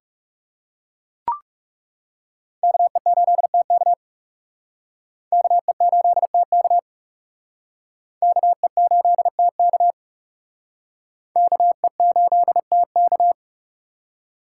Morse code, heard as a keyed tone, about 37 wpm, 700 Hz. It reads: KE8TK KE8TK KE8TK KE8TK
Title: Speed Racing - USA Call Signs 30wpm